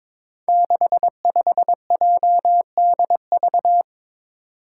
Morse code, heard as a keyed tone, about 22 wpm, 700 Hz. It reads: 65JDV
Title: Five Challenging Letter-Number Combinations 22wpm